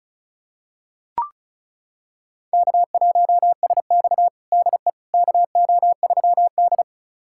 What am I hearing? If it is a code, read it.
Morse code, heard as a keyed tone, about 36 wpm, 700 Hz. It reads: K1SX DE KO3D
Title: USA Callsign DE Callsign - 4 Characters - 35wpm